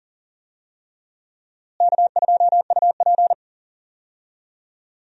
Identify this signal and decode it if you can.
Morse code, heard as a keyed tone, about 40 wpm, 700 Hz. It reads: K2UP
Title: USA Call Signs - 4 Characters - Repeated 3x - 40wpm